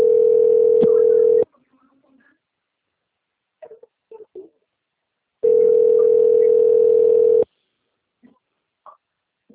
Ringback tone (North America), 440 + 480 Hz.
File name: out- - - - - .15609